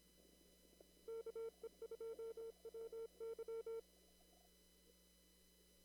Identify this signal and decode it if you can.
Morse code, heard as a keyed tone, about 25 wpm, 450 Hz. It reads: KE2WY
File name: CW / kt5j wpx cw-op-k5pi